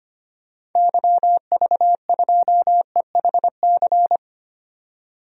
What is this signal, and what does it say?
Morse code, heard as a keyed tone, about 25 wpm, 700 Hz. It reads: YV2EHC
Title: DXCC-100 Call Signs - 6 Characters - 25wpm